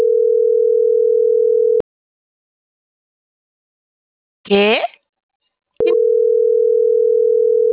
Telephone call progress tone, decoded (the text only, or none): ringback tone